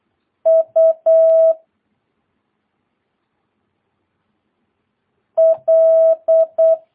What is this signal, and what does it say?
Morse code, heard as a keyed tone, about 8 wpm, 650 Hz. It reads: U L
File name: jh8cbh / 2022cw